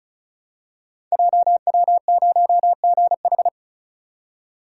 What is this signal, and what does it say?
Morse code, heard as a keyed tone, about 36 wpm, 700 Hz. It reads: JW0GH
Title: Code Groups - 5 Letter:Numbers - 35wpm